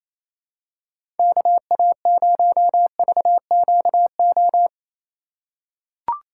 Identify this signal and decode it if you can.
Morse code, heard as a keyed tone, about 28 wpm, 700 Hz. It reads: KA0VQO
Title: USA Call Signs - 6 Characters - 4x Character Spacing - 28wpm